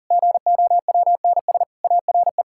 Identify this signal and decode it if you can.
Morse code, heard as a keyed tone, about 40 wpm, 700 Hz. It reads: GOWNS ARE
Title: Pangram Challenge v2 - 40wpm